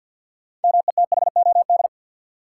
Morse code, heard as a keyed tone, about 53 wpm, 700 Hz. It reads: METHOD
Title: Sentences from Top 600 Words - New Word Drill - 50wpm